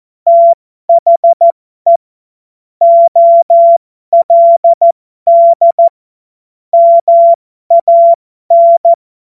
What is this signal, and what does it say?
Morse code, heard as a keyed tone, about 14 wpm, spaced as for 12 wpm, 680 Hz. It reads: THE OLD MAN
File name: Morsekurs / Witze